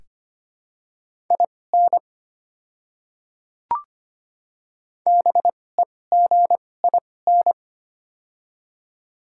Morse code, heard as a keyed tone, about 25 wpm, spaced as for 18 wpm, 700 Hz. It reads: IN BEGIN
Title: 2x Character Spacing - Top 300 Words - 25wpm